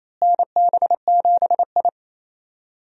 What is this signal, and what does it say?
Morse code, heard as a keyed tone, about 28 wpm, 700 Hz. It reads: NB7I
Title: USA Call Signs - 4 Characters - Repeated 3x - 28wpm